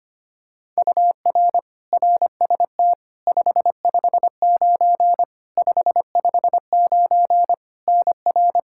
Morse code, heard as a keyed tone, about 25 wpm, 700 Hz. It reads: UR RST 559 559 NR
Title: Practice QSO - Exchange 1 - Beginning - 25wpm